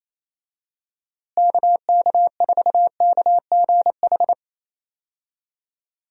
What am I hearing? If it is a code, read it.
Morse code, heard as a keyed tone, about 27 wpm, 700 Hz. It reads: KK4KGH
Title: USA Call Signs - 6 Characters - Repeated 2x - 28wpm